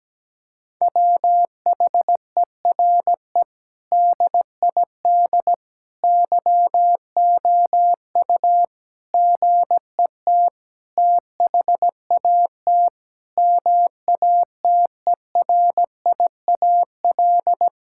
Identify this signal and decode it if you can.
Morse code, heard as a keyed tone, about 17 wpm, 700 Hz. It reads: WHERE DID YOU GET THAT MATERIAL
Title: Sentences from Top 1000 Words - Review Part 5 - 17wpm